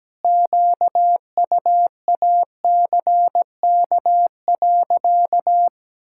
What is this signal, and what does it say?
Morse code, heard as a keyed tone, about 17 wpm, 700 Hz. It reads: QUACK.
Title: Pangram Challenge v2 - 17wpm